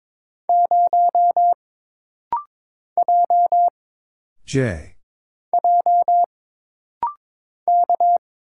Morse code, heard as a keyed tone, about 22 wpm, 700 Hz. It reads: 0 J J K
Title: Single Letter-Number - Rapid-Fire 22wpm